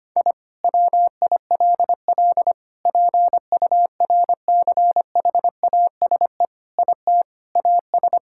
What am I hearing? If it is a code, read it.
Morse code, heard as a keyed tone, about 25 wpm, 700 Hz. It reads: I WILL PURCHASE IT AS
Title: Sentences from Top 1000 Words - Review Part 5 - 25wpm